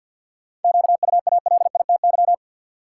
Morse code, heard as a keyed tone, about 51 wpm, 700 Hz. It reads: QUALITY
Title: Sentences from Top 1000 Words - Review Part 4 - New Word Drill - 50wpm